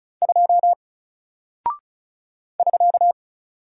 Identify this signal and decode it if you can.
Morse code, heard as a keyed tone, about 36 wpm, 700 Hz. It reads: 2 <SK>
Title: Single Character (Letter:Number:Punctuation:Prosign) Alt 35wpm